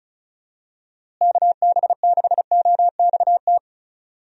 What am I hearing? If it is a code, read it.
Morse code, heard as a keyed tone, about 36 wpm, 700 Hz. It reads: KB6OXT
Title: USA Call Signs - 6 Characters - Repeated 2x - 35wpm